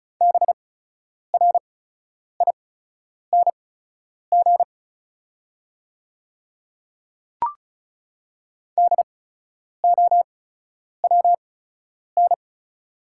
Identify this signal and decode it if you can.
Morse code, heard as a keyed tone, about 34 wpm, spaced as for 10 wpm, 690 Hz. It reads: BRING DOWN
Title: 8x Character Spacing - Top 500 Words - 35wpm